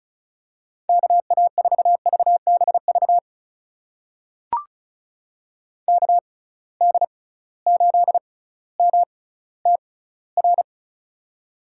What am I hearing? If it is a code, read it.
Morse code, heard as a keyed tone, about 34 wpm, spaced as for 18 wpm, 700 Hz. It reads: KA4VBV KD8MTR
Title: USA Call Signs - 6 Characters - 6x Character Spacing - 35wpm